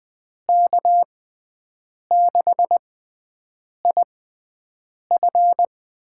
Morse code, heard as a keyed tone, about 20 wpm, spaced as for 7 wpm, 700 Hz. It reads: K6IF
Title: USA Call Signs - 6x Character Spacing - 20wpm